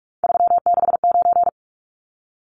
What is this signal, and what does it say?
Morse code, heard as a keyed tone, about 44 wpm, 700 Hz. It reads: 369
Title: Numbers - 3-Digit 45wpm